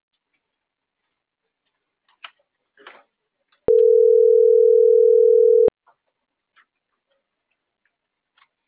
Ringback tone (North America), 440 + 480 Hz.
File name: CENTRAL MAERSK / out martin.blas